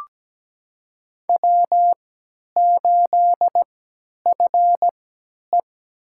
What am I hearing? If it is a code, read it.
Morse code, heard as a keyed tone, about 17 wpm, spaced as for 10 wpm, 700 Hz. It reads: W8FE
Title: USA Call Signs - 4 Characters - 3x Character Spacing - 17wpm